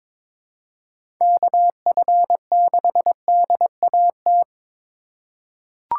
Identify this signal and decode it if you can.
Morse code, heard as a keyed tone, about 21 wpm, 700 Hz. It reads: KF6DAT
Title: USA Call Signs - 6 Characters - 4x Character Spacing - 22wpm